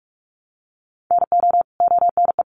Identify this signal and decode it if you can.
Morse code, heard as a keyed tone, about 44 wpm, 700 Hz. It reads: NO ONE
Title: Sentences from Top 100 Words 45wpm